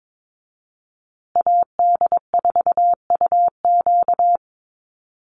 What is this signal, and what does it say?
Morse code, heard as a keyed tone, about 22 wpm, 700 Hz. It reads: AD4UQ
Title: USA Call Signs Repeated 3x 22wpm